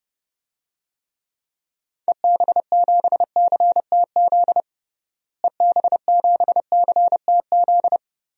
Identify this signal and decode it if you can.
Morse code, heard as a keyed tone, about 32 wpm, 700 Hz. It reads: EB7CTZ EB7CTZ
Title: DXCC-100 Call Signs - 6 Characters - Repeated 3x - 30wpm